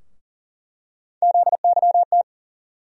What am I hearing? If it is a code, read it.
Morse code, heard as a keyed tone, about 40 wpm, 700 Hz. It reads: ZYT
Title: USA Call Sign Postfixes - 3 Letters 40wpm